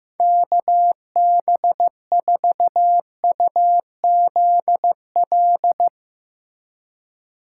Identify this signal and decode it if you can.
Morse code, heard as a keyed tone, about 15 wpm, 700 Hz. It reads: KB4UZL
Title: USA Call Signs - 6 Characters - 4x Character Spacing - 15wpm